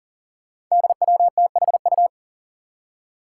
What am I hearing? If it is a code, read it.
Morse code, heard as a keyed tone, about 40 wpm, 700 Hz. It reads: DWTHU
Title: Five Challenging Letter-Number Combinations 40wpm